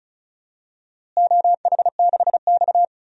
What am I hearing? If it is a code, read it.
Morse code, heard as a keyed tone, about 36 wpm, 700 Hz. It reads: OH6X
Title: DXCC-100 Call Signs - 4 Characters - Repeated 3x - 35wpm